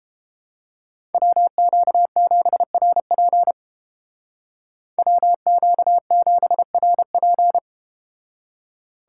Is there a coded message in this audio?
Yes: Morse code from 1.1 to 7.6 s, heard as a keyed tone, about 32 wpm, 700 Hz. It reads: WQ7RP WQ7RP